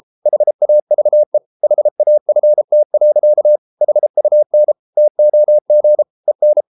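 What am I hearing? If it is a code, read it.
Morse code, heard as a keyed tone, about 33 wpm, 600 Hz. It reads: HAVE HAFT. HUN TOG EN
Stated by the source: Morse code at 33 wpm, QRQ.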